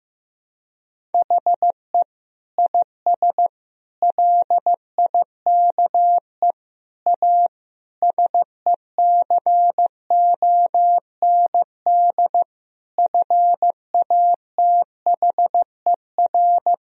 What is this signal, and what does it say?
Morse code, heard as a keyed tone, about 15 wpm, 700 Hz. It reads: HE IS LIKE A SECOND FATHER
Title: Sentences from Top 1000 Words - Review Part 1 - 15wpm